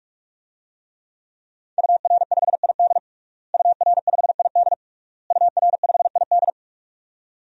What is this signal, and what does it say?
Morse code, heard as a keyed tone, about 44 wpm, 700 Hz. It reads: UR5ID UR5ID UR5ID
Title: DXCC-100 Call Signs - 5 Characters - Repeated 3x - 45wpm